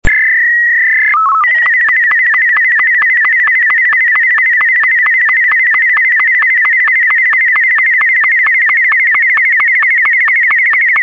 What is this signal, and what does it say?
SSTV (slow-scan television), Martin 2.
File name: sstv / sstv1 / martin m2